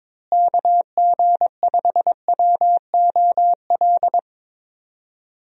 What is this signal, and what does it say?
Morse code, heard as a keyed tone, about 22 wpm, 700 Hz. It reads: KG5WOL
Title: USA Call Signs - 6 Characters - Repeated 3x - 22wpm